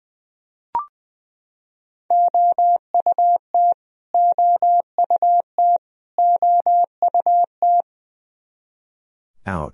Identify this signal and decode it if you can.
Morse code, heard as a keyed tone, about 20 wpm, 700 Hz. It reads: OUT OUT OUT
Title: Top 100 Words Repeated 20wpm